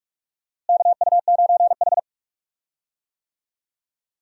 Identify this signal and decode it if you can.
Morse code, heard as a keyed tone, about 46 wpm, 700 Hz. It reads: KU9H
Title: USA Call Signs - 4 Characters - Repeated 2x - 45wpm